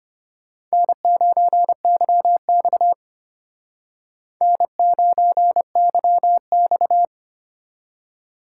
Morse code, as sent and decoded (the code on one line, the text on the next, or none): -. ----. -.-- -..- / -. ----. -.-- -..-
N9YX N9YX